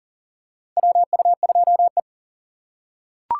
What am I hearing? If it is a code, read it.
Morse code, heard as a keyed tone, about 40 wpm, 700 Hz. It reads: WU2E
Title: USA Call Signs - 4 Characters - 3x Character Spacing - 40wpm